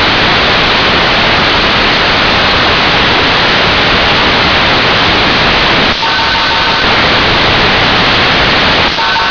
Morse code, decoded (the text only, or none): none